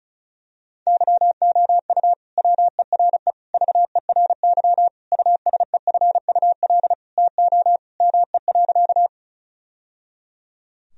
Morse code, heard as a keyed tone, about 34 wpm, 700 Hz. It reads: YOU WERE VERY USEFUL TO ME.
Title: Sentences from Top 1000 Words - Review Part 4 - 35wpm